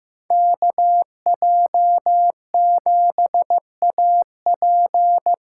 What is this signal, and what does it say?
Morse code, heard as a keyed tone, about 15 wpm, 700 Hz. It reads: KJ7AP
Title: USA Call Signs - 8x Character Spacing - 15wpm